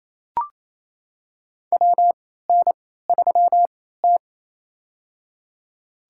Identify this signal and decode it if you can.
Morse code, heard as a keyed tone, about 28 wpm, spaced as for 16 wpm, 700 Hz. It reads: WN3T
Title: USA Call Signs - 4 Characters - 3x Character Spacing - 28wpm